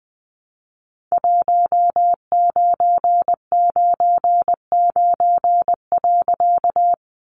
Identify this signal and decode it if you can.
Morse code, heard as a keyed tone, about 20 wpm, 700 Hz. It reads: 1999.
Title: A Sound Of Thunder 20wpm